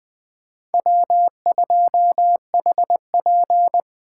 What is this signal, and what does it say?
Morse code, heard as a keyed tone, about 23 wpm, 700 Hz. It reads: W2HP
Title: Speed Racing - USA Call Signs 20wpm